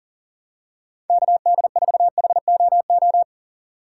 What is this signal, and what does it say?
Morse code, heard as a keyed tone, about 40 wpm, 700 Hz. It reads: KD4HOO
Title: USA Call Signs - 6 Characters - 4x Character Spacing - 40wpm